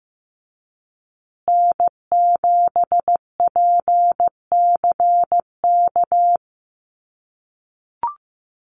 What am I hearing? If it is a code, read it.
Morse code, heard as a keyed tone, about 15 wpm, 700 Hz. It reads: N7PCK
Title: USA Call Signs Repeated 2x 15wpm